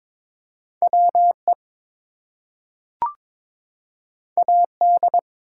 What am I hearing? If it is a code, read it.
Morse code, heard as a keyed tone, about 22 wpm, 700 Hz. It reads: WE AD
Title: Two Letter Words - 22wpm